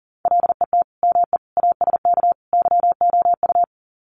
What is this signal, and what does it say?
Morse code, heard as a keyed tone, about 40 wpm, 700 Hz. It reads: LET ME ASK YOU
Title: Sets of 4 Words 40wpm